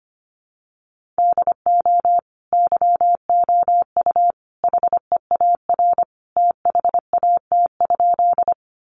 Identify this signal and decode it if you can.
Morse code, heard as a keyed tone, about 25 wpm, 700 Hz. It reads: DO YOU HEAR THAT?
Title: Sentences from Top 300 Words 25wpm